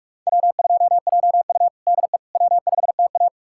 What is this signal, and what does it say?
Morse code, heard as a keyed tone, about 47 wpm, 690 Hz. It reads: W2JU DE W5TA
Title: USA Callsign DE Callsign - 4 Characters - 45wpm